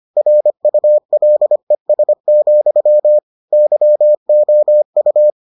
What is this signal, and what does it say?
Morse code, heard as a keyed tone, about 25 wpm, 600 Hz. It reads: RULES, YOU